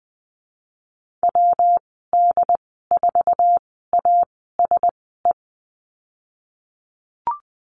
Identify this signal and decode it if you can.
Morse code, heard as a keyed tone, about 20 wpm, spaced as for 14 wpm, 700 Hz. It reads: WD4ASE
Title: 140 Lesson 23g - Letter f - Call Signs 20wpm